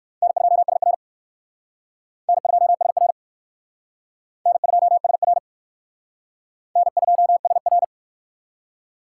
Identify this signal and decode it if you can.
Morse code, heard as a keyed tone, about 55 wpm, 700 Hz. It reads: N2SR N2SR N2SR N2SR
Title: Speed Racing - USA 1x2 Call Signs 45wpm